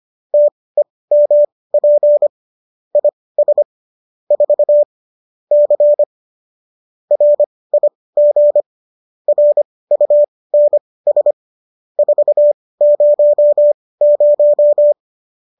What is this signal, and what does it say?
Morse code, heard as a keyed tone, about 25 wpm, spaced as for 18 wpm, 590 Hz. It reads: TEMP IS 4 C RIG RUNS 400
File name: qso218 18